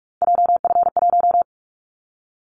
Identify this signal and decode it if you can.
Morse code, heard as a keyed tone, about 47 wpm, 700 Hz. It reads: JF1
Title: Callsign Prefixes for Top 25 DXCC - Repeated 3x - 45wpm